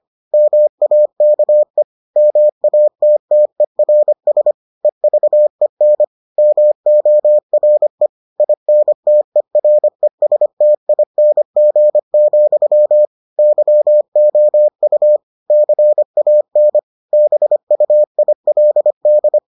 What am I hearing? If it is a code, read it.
Morse code, heard as a keyed tone, about 25 wpm, 600 Hz. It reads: MAKE MATTERS EVEN MORE INTERESTING, YOU CAN BUILD